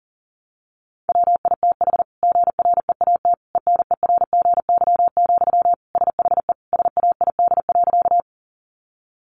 Morse code, heard as a keyed tone, about 40 wpm, 700 Hz. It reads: WITH GREAT ENERGY, SHE SAID.
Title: Sentences from Top 900 Words 40wpm